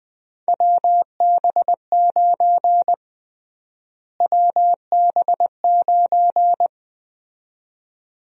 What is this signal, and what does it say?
Morse code, heard as a keyed tone, about 20 wpm, 700 Hz. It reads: WB9 WB9
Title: USA Call Sign Prefixes - Repeated 2x - 20wpm